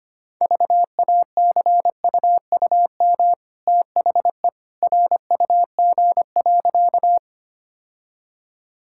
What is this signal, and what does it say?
Morse code, heard as a keyed tone, about 25 wpm, 700 Hz. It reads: VACUUM THE RUG.
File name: Easy Sentences with Hints - 25wpm